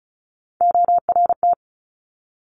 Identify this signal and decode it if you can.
Morse code, heard as a keyed tone, about 36 wpm, 700 Hz. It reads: ORT
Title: Top Three Letter Combinations in Words 35wpm